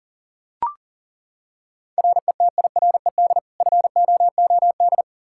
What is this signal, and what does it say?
Morse code, heard as a keyed tone, about 37 wpm, 700 Hz. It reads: RETIRED FOOD
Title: K1USN SST QSO Practice 40wpm